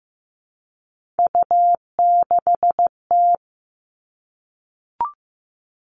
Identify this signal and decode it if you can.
Morse code, heard as a keyed tone, about 15 wpm, 700 Hz. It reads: U6T
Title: All Cut Numbers - 3-Digit 15wpm